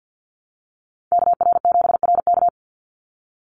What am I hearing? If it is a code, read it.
Morse code, heard as a keyed tone, about 48 wpm, 700 Hz. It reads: KR7RK
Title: NA SOTA Activator Callsigns 50wpm